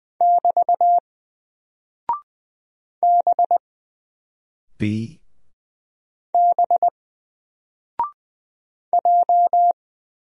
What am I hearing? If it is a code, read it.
Morse code, heard as a keyed tone, about 20 wpm, 700 Hz. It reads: <BT> B B J
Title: Single Character (Letter:Number:Punctuation:Prosign) Alt 20wpm